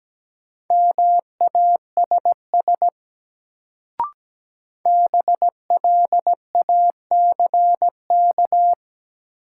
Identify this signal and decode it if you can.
Morse code, heard as a keyed tone, about 17 wpm, 700 Hz. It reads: MASS BLACK
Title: Sets of 5 Words Encore - New Word Drill - 17wpm